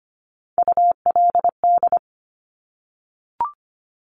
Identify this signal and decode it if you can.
Morse code, heard as a keyed tone, about 25 wpm, 700 Hz. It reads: ULD
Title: Top Three Letter Combinations in Words 25wpm